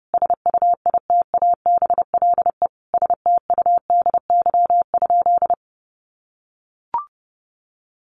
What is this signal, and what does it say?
Morse code, heard as a keyed tone, about 29 wpm, 700 Hz. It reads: SUITABLE STUDY?
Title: Sentences from Top 1400 Words 30wpm